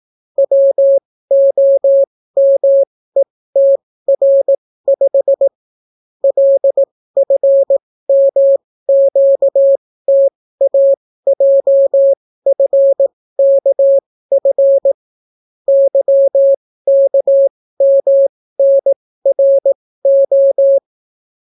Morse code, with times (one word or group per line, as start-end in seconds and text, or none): 0.4-5.5 WOMETR5
6.2-14.9 LFMQTAJFKF
15.7-20.8 YKMNRO